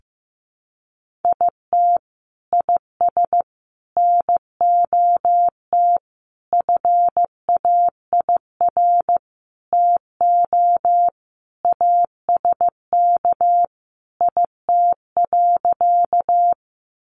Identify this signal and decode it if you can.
Morse code, heard as a keyed tone, about 15 wpm, 700 Hz. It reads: IT IS NOT FAIR TO ASK IT.
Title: Sentences from Top 1300 Words 15wpm